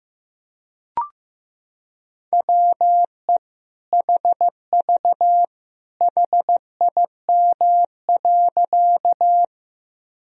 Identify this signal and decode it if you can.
Morse code, heard as a keyed tone, about 15 wpm, 700 Hz. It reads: WE HV HIM.
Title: Sentences from Top 100 Words and Abbreviations 15wpm